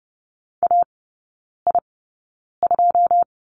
Morse code, heard as a keyed tone, about 30 wpm, spaced as for 9 wpm, 690 Hz. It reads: AI2
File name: Lesson 17e - Letter u - Three Characters 30wpm